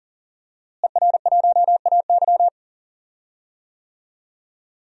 Morse code, heard as a keyed tone, about 40 wpm, 700 Hz. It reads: ER1AY
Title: DXCC-100 Call Signs - 5 Characters - Repeated 2x - 40wpm